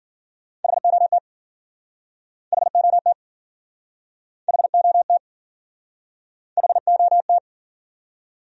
Morse code, heard as a keyed tone, about 52 wpm, 700 Hz. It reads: HOT HOT HOT HOT